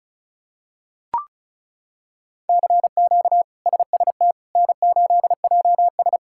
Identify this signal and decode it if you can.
Morse code, heard as a keyed tone, about 36 wpm, 700 Hz. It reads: CQ SST N8JS
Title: K1USN SST QSO Practice - 35wpm